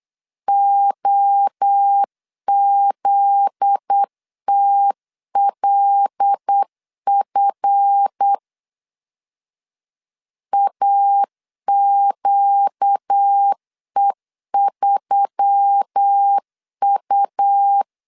Morse code, heard as a keyed tone, about 8 wpm, 800 Hz. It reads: OZTLF AQE3U
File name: LEZIONI / 8 YJ45 / 8 Y 8wpm 3245